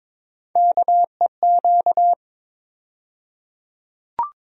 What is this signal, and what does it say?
Morse code, heard as a keyed tone, about 22 wpm, 700 Hz. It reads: KEQ
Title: USA Call Sign Postfixes - 3 Letters 22wpm